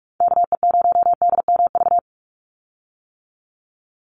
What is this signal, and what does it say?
Morse code, heard as a keyed tone, about 46 wpm, 700 Hz. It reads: KE0DMV
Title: USA Call Signs 45wpm